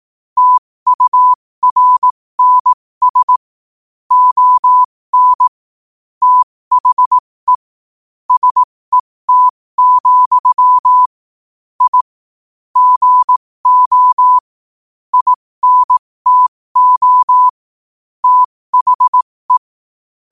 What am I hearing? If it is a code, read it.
Morse code, heard as a keyed tone, about 18 wpm, spaced as for 15 wpm, 1000 Hz. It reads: TURNS ON THE SET, I GO INTO THE
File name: qotdfiles / 15wpm